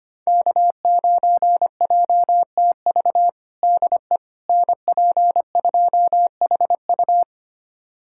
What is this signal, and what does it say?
Morse code, heard as a keyed tone, about 25 wpm, 700 Hz. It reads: K9JTV DE NP2HU
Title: USA Callsign DE Callsign - 5 Characters - 25wpm